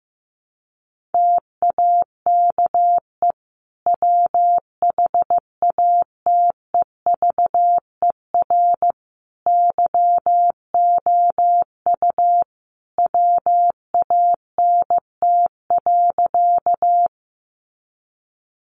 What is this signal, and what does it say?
Morse code, heard as a keyed tone, about 15 wpm, 700 Hz. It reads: TAKE WHATEVER YOU WANT.
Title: Sentences from Top 1100 Words 15wpm